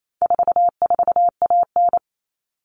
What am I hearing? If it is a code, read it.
Morse code, heard as a keyed tone, about 28 wpm, 700 Hz. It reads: 44AN
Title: All Cut Numbers - 4-Digit 28wpm